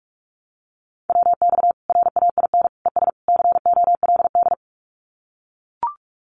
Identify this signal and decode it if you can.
Morse code, heard as a keyed tone, about 44 wpm, 700 Hz. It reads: WX RAIN ES COLD
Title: Common QSO Phrases 45wpm